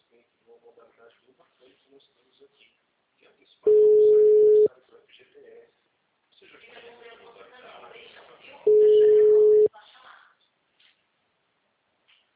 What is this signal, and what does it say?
Ringback tone (Europe, 425 Hz), with speech.
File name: out- -2008- - - .24416